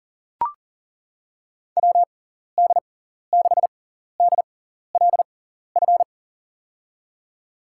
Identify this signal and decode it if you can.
Morse code, heard as a keyed tone, about 40 wpm, spaced as for 14 wpm, 700 Hz. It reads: WD6DLF
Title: USA Call Signs - 6 Characters - 6x Character Spacing - 40wpm